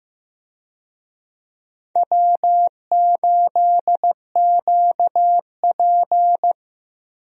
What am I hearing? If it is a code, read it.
Morse code, heard as a keyed tone, about 15 wpm, 700 Hz. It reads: W8QP
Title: USA Call Signs - 4 Characters - 15wpm